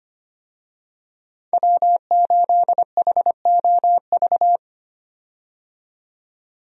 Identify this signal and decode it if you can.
Morse code, heard as a keyed tone, about 25 wpm, 690 Hz. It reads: W8HOV
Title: USA Call Signs - 5 Characters - 25wpm